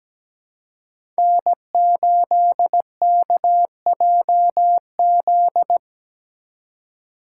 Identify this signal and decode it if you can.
Morse code, heard as a keyed tone, about 17 wpm, 700 Hz. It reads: N8KJZ
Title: Code Groups - 5 Letter:Numbers - 17wpm